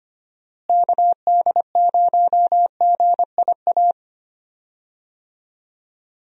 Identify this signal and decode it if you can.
Morse code, heard as a keyed tone, about 25 wpm, 700 Hz. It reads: KD0GIA
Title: USA Call Signs - 6 Characters - Repeated 3x - 25wpm